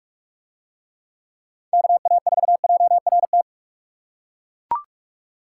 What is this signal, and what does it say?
Morse code, heard as a keyed tone, about 41 wpm, 700 Hz. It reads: KA4JRT